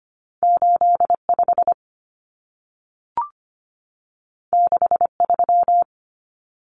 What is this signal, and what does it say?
Morse code, heard as a keyed tone, about 25 wpm, 700 Hz. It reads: 85 63
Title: Numbers - 2-Digit 25wpm